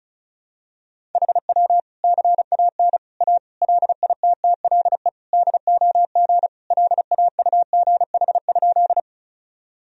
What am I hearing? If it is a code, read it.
Morse code, heard as a keyed tone, about 36 wpm, 700 Hz. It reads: HW CAN A LITTLE DOG LAUGH?
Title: Sentences from Top 500 Words and Abbreviations 35wpm